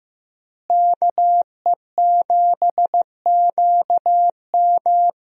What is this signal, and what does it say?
Morse code, heard as a keyed tone, about 15 wpm, 700 Hz. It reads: KE7QM
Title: USA Call Signs - 5 Characters - Repeated 2x - 15wpm